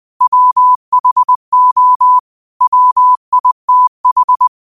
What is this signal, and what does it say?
Morse code, heard as a keyed tone, about 20 wpm, 1000 Hz. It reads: WHO WITH